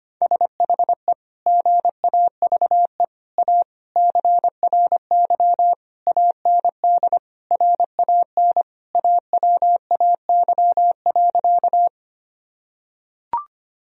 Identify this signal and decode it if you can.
Morse code, heard as a keyed tone, about 25 wpm, 700 Hz. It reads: SHE GAVE A CRY AND RAN AWAY.
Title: Sentences from Top 1000 Words - Review Part 3 - 25wpm